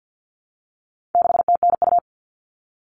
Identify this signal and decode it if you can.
Morse code, heard as a keyed tone, about 50 wpm, 700 Hz. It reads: 6TNU
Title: All Cut Numbers - 4-Digit 50wpm